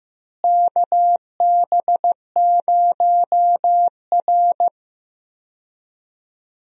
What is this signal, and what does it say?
Morse code, heard as a keyed tone, about 15 wpm, 700 Hz. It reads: KB0R
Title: USA Call Signs - 4 Characters - Repeated 3x - 15wpm